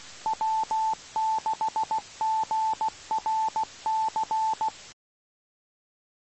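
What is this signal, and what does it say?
Morse code, heard as a keyed tone, about 16 wpm, 880 Hz. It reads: W6GRC
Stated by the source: Morse received on the 2 m band, 147.105 MHz.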